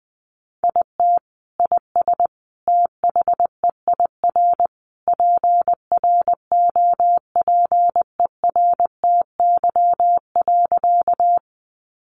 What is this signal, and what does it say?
Morse code, heard as a keyed tone, about 20 wpm, 700 Hz. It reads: IT IS THEIR PROPERTY.
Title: Sentences from Top 1000 Words 20wpm